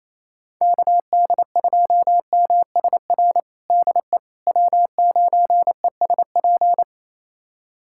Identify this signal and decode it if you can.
Morse code, heard as a keyed tone, about 28 wpm, 700 Hz. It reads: KD2MSR DE W9ESP
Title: USA Callsign DE Callsign 28wpm